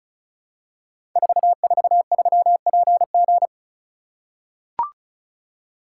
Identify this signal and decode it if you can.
Morse code, heard as a keyed tone, about 36 wpm, 690 Hz. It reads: 443PG
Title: Code Groups - 5 Letter:Numbers - 35wpm